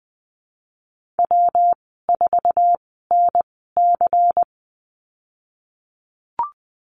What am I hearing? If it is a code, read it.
Morse code, heard as a keyed tone, about 20 wpm, spaced as for 15 wpm, 700 Hz. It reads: W4NC
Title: 154 Lesson 25g - Letter p - Call Signs 20wpm